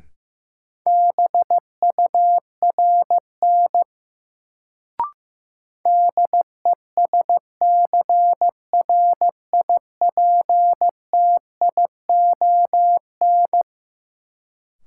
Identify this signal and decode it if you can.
Morse code, heard as a keyed tone, about 15 wpm, 700 Hz. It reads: BURN DESCRIPTION
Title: Sentences from Top 1300 Words - New Word Drill - 15wpm